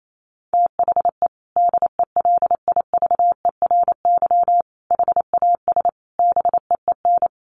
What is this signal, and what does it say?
Morse code, heard as a keyed tone, about 28 wpm, 700 Hz. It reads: THE DELIVERY HAS BEEN
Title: Sentences from Top 1500 Words 28wpm